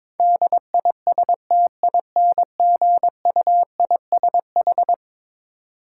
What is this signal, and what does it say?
Morse code, heard as a keyed tone, about 22 wpm, 700 Hz. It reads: DISTINGUISH
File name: practice / 22wpm / Eleven Letter Words - 22wpm